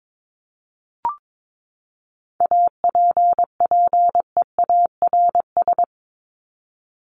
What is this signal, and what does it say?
Morse code, heard as a keyed tone, about 22 wpm, 700 Hz. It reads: APPEARS
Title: ES:S Suffix - Top 100 Words with Base Words 22wpm